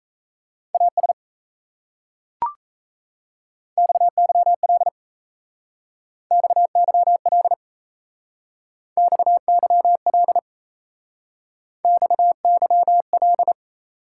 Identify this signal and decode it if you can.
Morse code, heard as a keyed tone, about 35 wpm, 700 Hz. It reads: AS XYL XYL XYL XYL